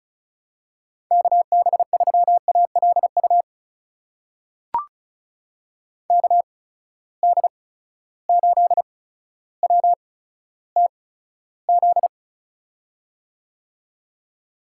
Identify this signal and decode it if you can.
Morse code, heard as a keyed tone, about 36 wpm, spaced as for 15 wpm, 700 Hz. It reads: KB3ALU KD8WTZ
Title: USA Call Signs - 6 Characters - 8x Character Spacing - 35wpm